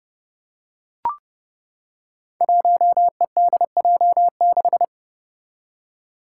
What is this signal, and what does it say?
Morse code, heard as a keyed tone, about 30 wpm, 700 Hz. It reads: 1EDJ6
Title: Five Challenging Letter-Number Combinations 30wpm